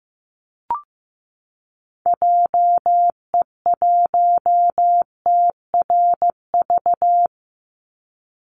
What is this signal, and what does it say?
Morse code, heard as a keyed tone, about 15 wpm, 700 Hz. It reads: JE1TRV
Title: Call Signs Repeated 3x - 15wpm